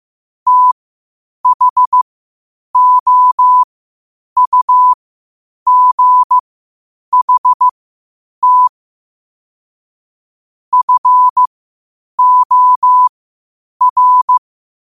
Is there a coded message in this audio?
Yes: Morse code from 0.5 to 14.4 s, heard as a keyed tone, about 15 wpm, spaced as for 8 wpm, 1000 Hz. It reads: THOUGHT FOR